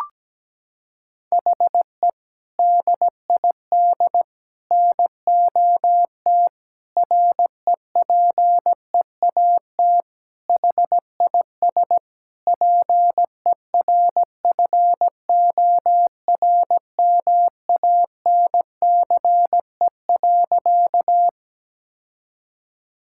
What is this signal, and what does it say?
Morse code, heard as a keyed tone, about 17 wpm, 700 Hz. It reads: HE DID NOT REPEAT HIS PERFORMANCE.
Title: Sentences from Top 1600 Words 17wpm